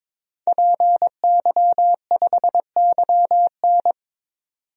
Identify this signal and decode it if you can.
Morse code, heard as a keyed tone, about 22 wpm, 700 Hz. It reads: PY5YN
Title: DXCC-100 Call Signs - 5 Characters - 22wpm